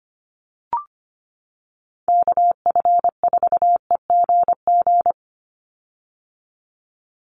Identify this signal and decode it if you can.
Morse code, heard as a keyed tone, about 25 wpm, 700 Hz. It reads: KF4EGG
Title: USA Call Signs 25wpm